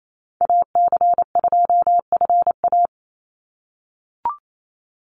Morse code, heard as a keyed tone, about 28 wpm, 700 Hz. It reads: AC2FA
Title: Call Signs - 28wpm